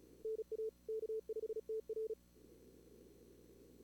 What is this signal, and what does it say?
Morse code, heard as a keyed tone, about 36 wpm, 440 Hz. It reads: NA K5TR
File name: CWSprint / Feb2022 / k5tr-feb-na-sprint-cw-k5ot -2300